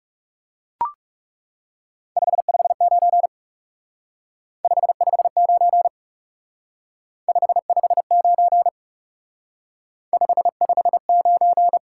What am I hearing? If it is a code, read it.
Morse code, heard as a keyed tone, about 38 wpm, 700 Hz. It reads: 559 559 559 559
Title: Speed-Racing Top 180 QSO Elements 30wpm